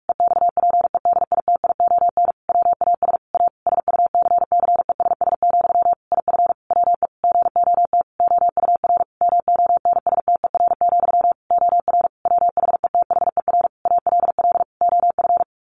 Morse code, heard as a keyed tone, about 44 wpm, 700 Hz. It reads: EXPEDITION WAS A SUCCESS, IF WE GOT OUR MONSTER, OR WHETHER ALL OF
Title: A Sound Of Thunder 45wpm